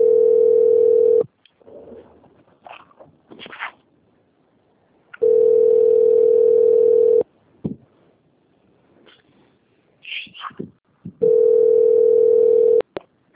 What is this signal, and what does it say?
Ringback tone (North America), 440 + 480 Hz.
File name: out- - - - - .21849